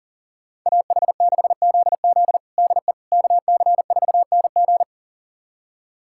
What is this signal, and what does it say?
Morse code, heard as a keyed tone, about 40 wpm, 700 Hz. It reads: AH6ZZ DE KC4NG
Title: USA Callsign DE Callsign - 5 Characters - 40wpm